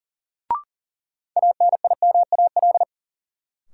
Morse code, heard as a keyed tone, about 40 wpm, 700 Hz. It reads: ANIMAL
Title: Sentences from Top 1000 Words - Review Part 1 - New Word Drill - 40wpm